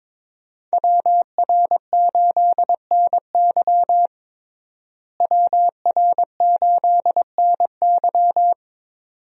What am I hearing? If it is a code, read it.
Morse code, heard as a keyed tone, about 22 wpm, 700 Hz. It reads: WR8NY WR8NY
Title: USA Call Signs - 5 Characters - Repeated 2x - 22wpm